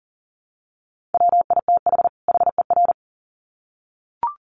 Morse code, heard as a keyed tone, about 40 wpm, 700 Hz. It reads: WITH HER
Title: Sets of 2 Words 40wpm